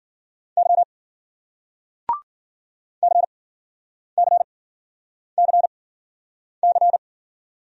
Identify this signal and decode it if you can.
Morse code, heard as a keyed tone, about 50 wpm, spaced as for 9 wpm, 700 Hz. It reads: Y CCCC